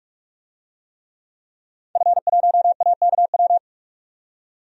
Morse code, heard as a keyed tone, about 41 wpm, 700 Hz. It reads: F1AKW
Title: DXCC-100 Call Signs - 5 Characters - 45wpm